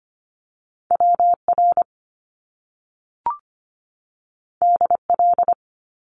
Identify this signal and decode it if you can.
Morse code, heard as a keyed tone, about 25 wpm, 700 Hz. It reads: WR DL
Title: Top Two Letter Combinations in Words 25wpm